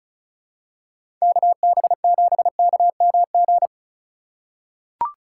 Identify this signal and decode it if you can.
Morse code, heard as a keyed tone, about 36 wpm, 700 Hz. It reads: KB7KMG
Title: USA Call Signs - 6 Characters - 2x Character Spacing - 35wpm